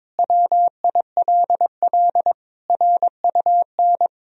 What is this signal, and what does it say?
Morse code, heard as a keyed tone, about 22 wpm, 700 Hz. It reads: WILL RUN
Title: Sentences from Top 1000 Words - Review Part 1 - 22wpm